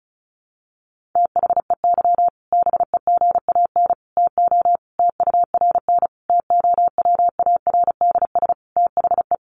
Morse code, heard as a keyed tone, about 34 wpm, 700 Hz. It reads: THEY BEGAN TO TURN TOWARDS THE
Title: Sentences from Top 1100 Words 35wpm